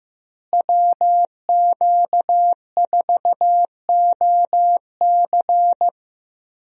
Morse code, heard as a keyed tone, about 15 wpm, 700 Hz. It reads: WQ4OC
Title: USA Call Signs - 5 Characters - Repeated 3x - 15wpm